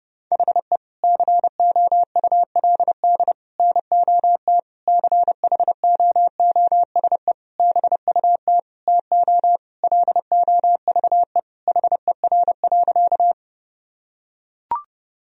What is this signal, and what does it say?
Morse code, heard as a keyed tone, about 30 wpm, 700 Hz. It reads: HE COULD NOT CHOOSE BUT TO LOVE HER.
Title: Sentences from Top 1000 Words - Review Part 4 - 30wpm